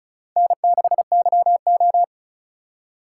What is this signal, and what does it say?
Morse code, heard as a keyed tone, about 34 wpm, 700 Hz. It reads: N6YO
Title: USA Call Signs - 4 Characters - 2x Character Spacing - 35wpm